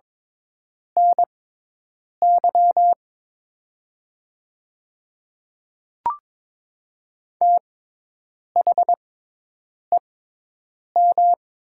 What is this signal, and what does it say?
Morse code, heard as a keyed tone, about 22 wpm, spaced as for 8 wpm, 700 Hz. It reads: NY THEM
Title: 6x Character Spacing - Top 200 Words - 22wpm